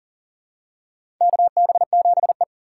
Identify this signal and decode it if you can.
Morse code, heard as a keyed tone, about 40 wpm, 700 Hz. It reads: KB7E
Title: USA Call Signs - 2x Character Spacing - 40wpm